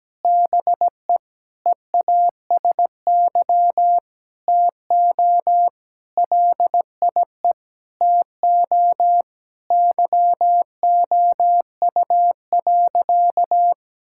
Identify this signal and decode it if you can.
Morse code, heard as a keyed tone, about 17 wpm, 700 Hz. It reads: BE EASY TO LIE TO YOU.
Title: Sentences from Top 1600 Words 17wpm